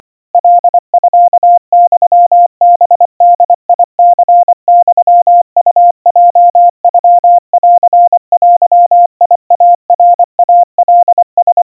Morse code (Japanese Wabun code), heard as a keyed tone, about 24 wpm, 700 Hz. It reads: カミユハボニユウヲノンデイナイカラ
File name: 90 cpm 700 Hz Crispy